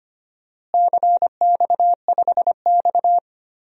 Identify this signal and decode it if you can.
Morse code, heard as a keyed tone, about 25 wpm, 700 Hz. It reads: CX5X
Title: DXCC-100 Call Signs - 4 Characters - Repeated 2x - 25wpm